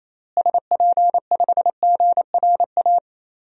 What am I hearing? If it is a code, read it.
Morse code, heard as a keyed tone, about 28 wpm, 700 Hz. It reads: SP5GRA
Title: DXCC-100 Call Signs - 6 Characters - Repeated 2x - 28wpm